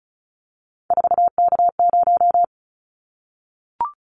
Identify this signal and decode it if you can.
Morse code, heard as a keyed tone, about 36 wpm, 700 Hz. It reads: 4K0
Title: Callsign Prefixes for Top 100 DXCC - 35wpm